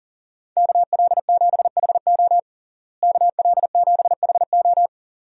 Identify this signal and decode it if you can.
Morse code, heard as a keyed tone, about 40 wpm, 700 Hz. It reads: KL7HO KL7HO
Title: DXCC-100 Call Signs - 5 Characters - Repeated 2x - 40wpm